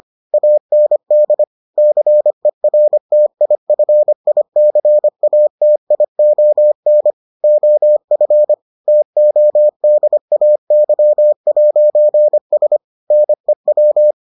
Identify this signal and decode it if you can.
Morse code, heard as a keyed tone, about 25 wpm, 600 Hz. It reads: AND CERTIFICATION OF TODAY'S NEW